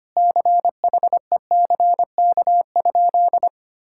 Morse code, heard as a keyed tone, about 25 wpm, 700 Hz. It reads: CHECK?
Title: Sentences from Top 1000 Words - Review Part 6 - 25wpm